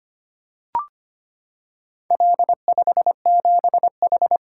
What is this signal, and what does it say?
Morse code, heard as a keyed tone, about 25 wpm, 700 Hz. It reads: L57H
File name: practice / 25wpm / Four Letter-Number Combinations - 25wpm